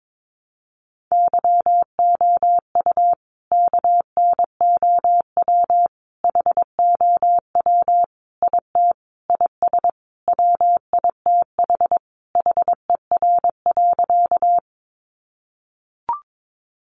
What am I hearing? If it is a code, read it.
Morse code, heard as a keyed tone, about 22 wpm, 700 Hz. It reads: YOU KNOW HOW IT IS WITH HER.
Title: Sentences from Top 100 Words 22wpm